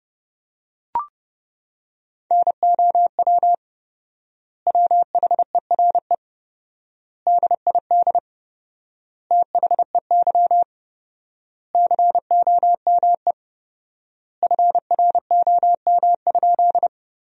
Morse code, heard as a keyed tone, about 29 wpm, 700 Hz. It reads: NOW WHERE DID THEY COME FROM?
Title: 4x Word-Spacing - Sentences from Top 200 Words - 30wpm